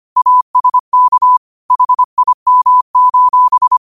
Morse code, heard as a keyed tone, about 25 wpm, 1000 Hz. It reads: ASK HIM: